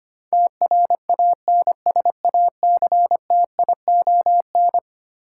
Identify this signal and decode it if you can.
Morse code, heard as a keyed tone, about 25 wpm, 700 Hz. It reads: TRANSACTION
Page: morse-code-ninja-practice-25wpm